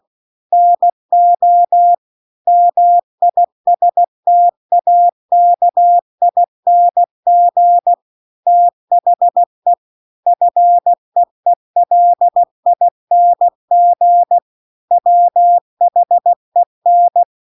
Morse code, heard as a keyed tone, about 16 wpm, 700 Hz. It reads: NO MISTAKING THE FEELING WHEN